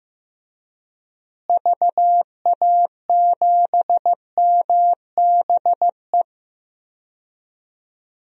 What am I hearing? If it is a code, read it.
Morse code, heard as a keyed tone, about 15 wpm, 700 Hz. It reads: VA7MBE
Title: DXCC-100 Call Signs - 6 Characters - Repeated 2x - 15wpm